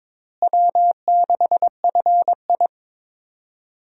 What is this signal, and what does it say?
Morse code, heard as a keyed tone, about 22 wpm, 700 Hz. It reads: W6FI
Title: USA Call Signs - 4 Characters - Repeated 3x - 22wpm